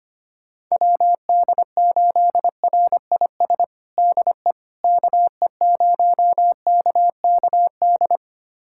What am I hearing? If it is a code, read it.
Morse code, heard as a keyed tone, about 25 wpm, 700 Hz. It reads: WD8RIS DE KE0KKD
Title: USA Callsign DE Callsign - 6 Characters - 25wpm